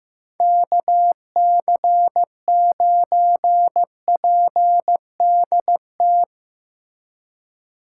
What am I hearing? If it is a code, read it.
Morse code, heard as a keyed tone, about 15 wpm, 700 Hz. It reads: KC9PDT
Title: USA Call Signs - 3x Character Spacing - 15wpm